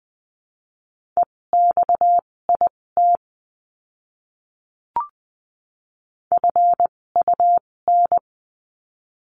Morse code, heard as a keyed tone, about 20 wpm, spaced as for 16 wpm, 700 Hz. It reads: EXIT FUN
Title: 256 Lesson 42a - Words 12-20wpm